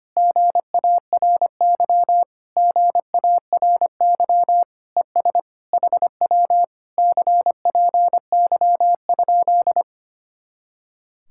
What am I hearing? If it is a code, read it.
Morse code, heard as a keyed tone, about 25 wpm, 700 Hz. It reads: GARY GARY ES HW CPY?
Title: Speed-Racing Common QSO Phrases Encore - 25wpm